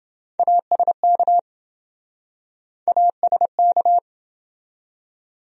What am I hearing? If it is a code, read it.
Morse code, heard as a keyed tone, about 28 wpm, 700 Hz. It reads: ASK ASK